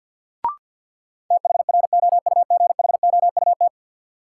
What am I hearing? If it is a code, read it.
Morse code, heard as a keyed tone, about 52 wpm, 700 Hz. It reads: THROUGHOUT
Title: Sentences from Top 800 Words - New Word Drill - 50wpm